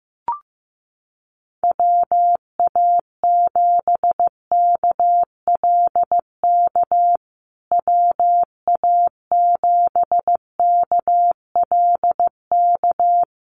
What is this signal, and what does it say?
Morse code, heard as a keyed tone, about 15 wpm, 700 Hz. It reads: WA7KLK WA7KLK
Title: USA Call Signs Repeated 3x 15wpm